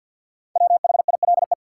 Morse code, heard as a keyed tone, about 48 wpm, 700 Hz. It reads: WHILE
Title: Sets of 3 Words - New Word Drill - 50wpm